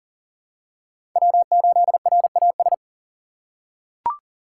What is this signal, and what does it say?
Morse code, heard as a keyed tone, about 40 wpm, 700 Hz. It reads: W8RAS
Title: USA Call Signs - 5 Characters - 2x Character Spacing - 40wpm